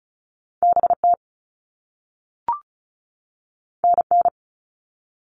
Morse code, heard as a keyed tone, about 36 wpm, 700 Hz. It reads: BT NN
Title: All Cut Numbers - 2-Digit 35wpm